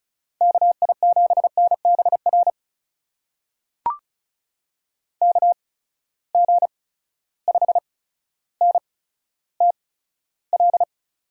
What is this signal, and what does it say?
Morse code, heard as a keyed tone, about 36 wpm, spaced as for 15 wpm, 700 Hz. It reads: KI7NBR KG5NTL
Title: USA Call Signs - 8x Character Spacing - 35wpm